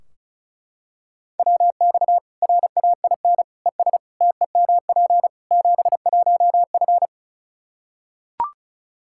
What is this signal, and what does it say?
Morse code, heard as a keyed tone, about 36 wpm, 700 Hz. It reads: WX RAIN ES TEMP 71F
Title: K1USN SST QSO Practice 35wpm